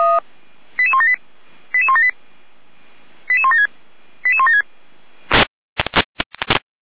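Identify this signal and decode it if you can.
Five-tone selective call (ZVEI), heard as quick runs of five tones, tones of about 70 ms. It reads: 88178 88178 88176 88176
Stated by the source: ZVEI five-tone selective call 88352 received by SDR.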